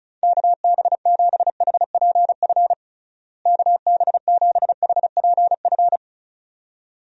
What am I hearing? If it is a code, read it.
Morse code, heard as a keyed tone, about 34 wpm, 700 Hz. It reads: KB7HPF KB7HPF
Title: USA Call Signs - 6 Characters - Repeated 2x - 35wpm